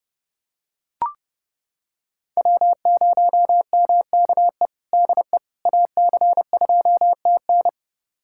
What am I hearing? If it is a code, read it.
Morse code, heard as a keyed tone, about 30 wpm, 700 Hz. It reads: W0MKE DE AC2TN
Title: USA Callsign DE Callsign - 5 Characters - 30wpm